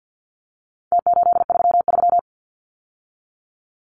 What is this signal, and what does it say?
Morse code, heard as a keyed tone, about 50 wpm, 700 Hz. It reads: T833
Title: Numbers - Contest Style - 4-Digit 50wpm